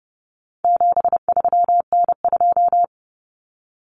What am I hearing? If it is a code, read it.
Morse code, heard as a keyed tone, about 30 wpm, 700 Hz. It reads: 73N2
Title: Common Cut Numbers - 4-Digit 30wpm